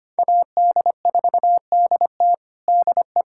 Morse code, heard as a keyed tone, about 25 wpm, 700 Hz. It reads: AD4DT DE
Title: USA Callsign DE Callsign - 5 Characters - 25wpm